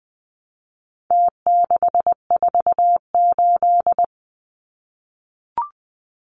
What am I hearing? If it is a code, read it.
Morse code, heard as a keyed tone, about 20 wpm, 700 Hz. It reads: T648
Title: Numbers - Contest Style - 4-Digit 20wpm